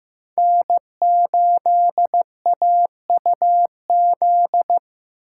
Morse code, heard as a keyed tone, about 15 wpm, 700 Hz. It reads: N8AUZ
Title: USA Call Signs - 5 Characters - 2x Character Spacing - 15wpm